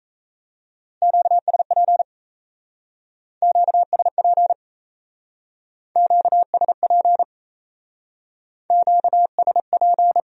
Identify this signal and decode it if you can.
Morse code, heard as a keyed tone, about 35 wpm, 700 Hz. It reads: QSP QSP QSP QSP
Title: Speed-Racing Q-Codes - 28wpm